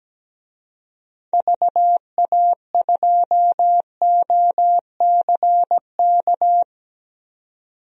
Morse code, heard as a keyed tone, about 17 wpm, 700 Hz. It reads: VA2OCK
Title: DXCC-100 Call Signs - 6 Characters - 17wpm